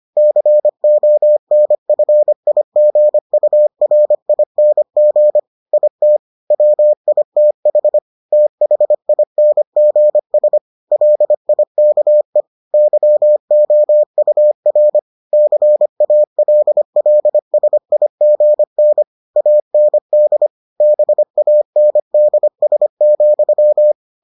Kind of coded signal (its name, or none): Morse code